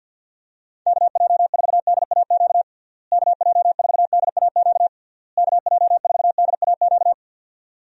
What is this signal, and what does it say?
Morse code, heard as a keyed tone, about 50 wpm, 700 Hz. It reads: KJ4DAQ KJ4DAQ KJ4DAQ
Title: USA Call Signs - 6 Characters - Repeated 3x - 50wpm